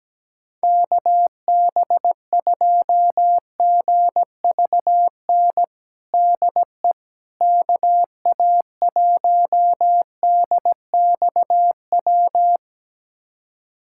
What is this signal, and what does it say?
Morse code, heard as a keyed tone, about 17 wpm, 700 Hz. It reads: KB2GVN DE KA1DXW
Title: USA Callsign DE Callsign - 6 Characters - 17wpm